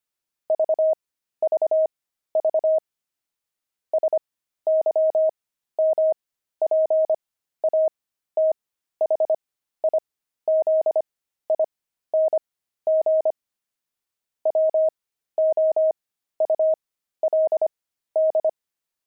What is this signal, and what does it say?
Morse code, heard as a keyed tone, about 25 wpm, spaced as for 13 wpm, 630 Hz. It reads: VVV SYMPATHIZING WOULD
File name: pangram-fs-13-cs-25